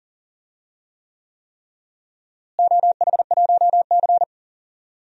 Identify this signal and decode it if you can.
Morse code, heard as a keyed tone, about 40 wpm, 700 Hz. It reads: OH1C
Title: DXCC-100 Call Signs - 4 Characters - Repeated 3x - 40wpm